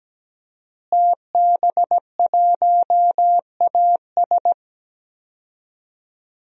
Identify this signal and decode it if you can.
Morse code, heard as a keyed tone, about 17 wpm, 700 Hz. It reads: TB1AS
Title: DXCC-100 Call Signs - 5 Characters - Repeated 2x - 17wpm